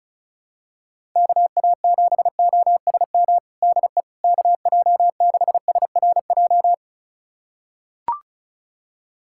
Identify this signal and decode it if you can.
Morse code, heard as a keyed tone, about 36 wpm, 700 Hz. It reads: KA7OSM DE KJ6SRJ
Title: USA Callsign DE Callsign 35wpm